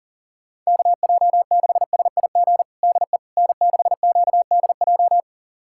Morse code, heard as a keyed tone, about 40 wpm, 700 Hz. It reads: KJ6SIG DE N6QDJ
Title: USA Callsign DE Callsign 40wpm